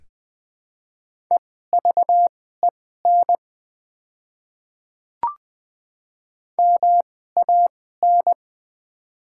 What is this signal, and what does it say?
Morse code, heard as a keyed tone, about 20 wpm, spaced as for 14 wpm, 700 Hz. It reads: EVEN MAN
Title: 2x Character Spacing - Top 200 Words - 20wpm